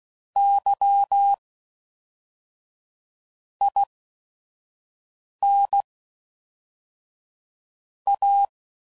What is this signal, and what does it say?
Morse code, heard as a keyed tone, about 16 wpm, spaced as for 5 wpm, 800 Hz. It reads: Y IN A
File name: CW Code Groups @5WPM / Technical Subjects / AND Gate